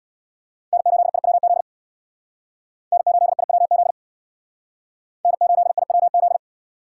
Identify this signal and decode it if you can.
Morse code, heard as a keyed tone, about 60 wpm, 700 Hz. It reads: N8IWZ N8IWZ N8IWZ
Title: Speed Racing - USA 1x3 Call Signs 50wpm